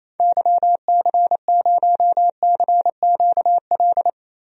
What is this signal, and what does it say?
Morse code, heard as a keyed tone, about 28 wpm, 700 Hz. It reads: YC0CQL
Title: DXCC-100 Call Signs - 6 Characters - Repeated 3x - 28wpm